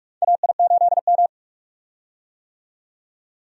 Morse code, heard as a keyed tone, about 46 wpm, 700 Hz. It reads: AI8M
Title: USA Call Signs - 4 Characters - 45wpm